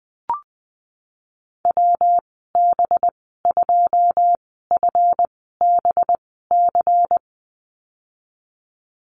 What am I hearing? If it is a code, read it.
Morse code, heard as a keyed tone, about 20 wpm, spaced as for 15 wpm, 700 Hz. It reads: WB2FBC
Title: 213 Lesson 34g - Number 8 - Call Signs 20wpm